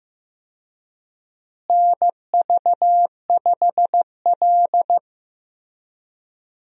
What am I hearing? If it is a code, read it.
Morse code, heard as a keyed tone, about 15 wpm, 700 Hz. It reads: NV5L
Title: USA Call Signs - 4 Characters - Repeated 3x - 15wpm